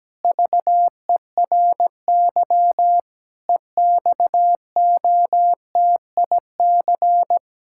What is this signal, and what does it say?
Morse code, heard as a keyed tone, about 17 wpm, 700 Hz. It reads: VERY EXOTIC
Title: Pangram Challenge v2 - 17wpm